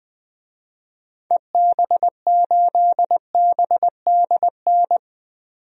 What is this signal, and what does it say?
Morse code, heard as a keyed tone, about 21 wpm, 700 Hz. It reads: EB8BDN
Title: DXCC-100 Call Signs - 6 Characters - 20wpm